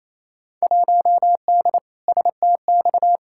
Morse code, heard as a keyed tone, about 28 wpm, 700 Hz. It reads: 1D STX
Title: Field Day Exchanges Encore - 28wpm